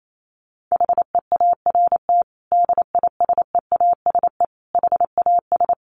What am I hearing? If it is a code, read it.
Morse code, heard as a keyed tone, about 28 wpm, 700 Hz. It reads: HEART DISEASE HAS
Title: Sentences from Top 1200 Words 28wpm